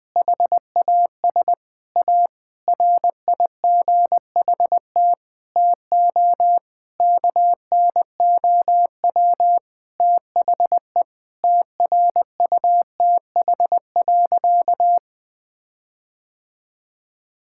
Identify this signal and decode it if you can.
Morse code, heard as a keyed tone, about 20 wpm, 700 Hz. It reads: HAS A RIGHT TO KNOW THE TRUTH.
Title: Sentences with 7-Word Phrases - 20wpm